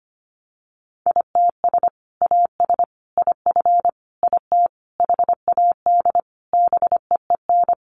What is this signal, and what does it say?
Morse code, heard as a keyed tone, about 25 wpm, 700 Hz. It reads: ITS AS IF IT HAD BEEN
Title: Sentences from Top 800 Words 25wpm